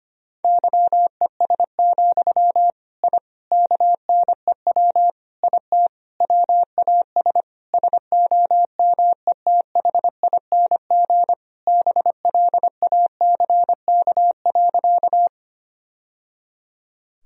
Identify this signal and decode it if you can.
Morse code, heard as a keyed tone, about 25 wpm, 700 Hz. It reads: YES, I KNEW IT WAS SOMETHING BLACK.
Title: Sentences from Top 1000 Words - Review Part 2 - 25wpm